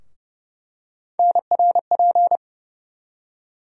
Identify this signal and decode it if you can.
Morse code, heard as a keyed tone, about 30 wpm, 700 Hz. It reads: NRP
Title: USA Call Sign Postfixes - 3 Letters 30wpm